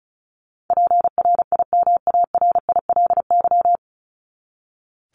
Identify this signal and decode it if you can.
Morse code, heard as a keyed tone, about 34 wpm, 700 Hz. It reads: PRIMARILY
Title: LY Suffix - Top 100 Words with Base Words 35wpm